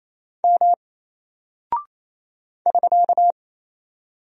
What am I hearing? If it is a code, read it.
Morse code, heard as a keyed tone, about 28 wpm, 700 Hz. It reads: M <SK>
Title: Single Character (Letter:Number:Punctuation:Prosign) Alt 28wpm